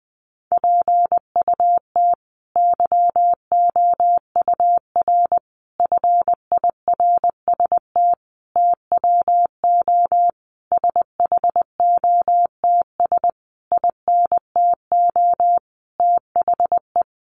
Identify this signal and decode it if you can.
Morse code, heard as a keyed tone, about 20 wpm, 700 Hz. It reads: PUT YOUR FIRST TWO SHOTS INTO THE
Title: A Sound Of Thunder 20wpm